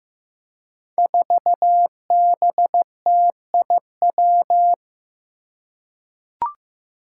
Morse code, heard as a keyed tone, about 15 wpm, 700 Hz. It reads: 4BTIW
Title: Five Challenging Letter-Number Combinations 15wpm